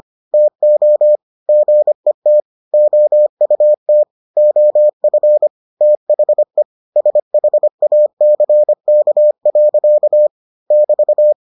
Morse code, heard as a keyed tone, about 25 wpm, 600 Hz. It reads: TO GET OUT OF THE SHACK. <BT>